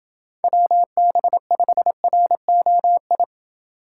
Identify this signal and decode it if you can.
Morse code, heard as a keyed tone, about 27 wpm, 700 Hz. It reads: WB5ROI